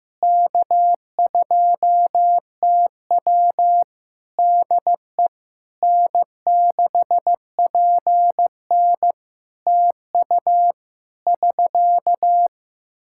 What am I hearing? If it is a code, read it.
Morse code, heard as a keyed tone, about 16 wpm, 700 Hz. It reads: K2TW DE N6PN TU <SK>
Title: Practice QSO - Exchange 4 - Ending - 15wpm